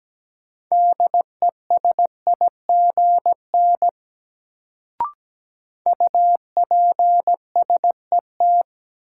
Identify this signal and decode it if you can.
Morse code, heard as a keyed tone, about 17 wpm, 700 Hz. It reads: DESIGN UPSET
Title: Sentences from Top 800 Words - New Word Drill - 17wpm